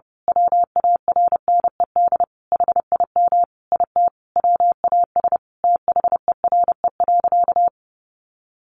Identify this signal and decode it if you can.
Morse code, heard as a keyed tone, about 30 wpm, 700 Hz. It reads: WARNED HIM IT WAS THERE.
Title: A Sound Of Thunder 30wpm